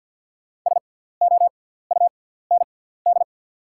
Morse code, heard as a keyed tone, about 48 wpm, spaced as for 17 wpm, 700 Hz. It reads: SOUND
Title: 6x Character Spacing - Top 200 Words - 50wpm